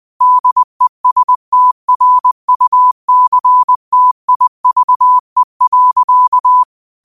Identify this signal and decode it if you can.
Morse code, heard as a keyed tone, about 20 wpm, 1000 Hz. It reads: DESTRUCTIVE.
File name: qotdfiles / 20wpm